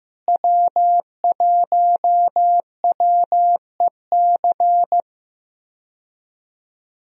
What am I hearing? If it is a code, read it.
Morse code, heard as a keyed tone, about 15 wpm, 700 Hz. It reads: W1WEC
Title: USA Call Signs - 5 Characters - Repeated 3x - 15wpm